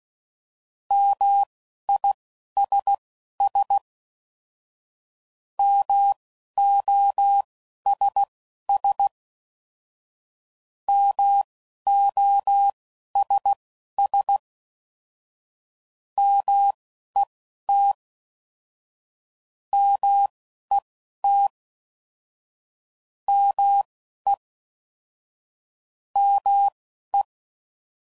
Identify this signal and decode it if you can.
Morse code, heard as a keyed tone, about 16 wpm, spaced as for 11 wpm, 800 Hz. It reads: MISS MOSS MOSS MET MET ME ME